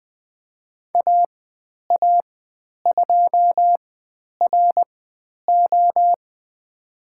Morse code, heard as a keyed tone, about 20 wpm, spaced as for 10 wpm, 700 Hz. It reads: AA2RO
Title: 095 Lesson 16g - Letter c - Call Signs 20wpm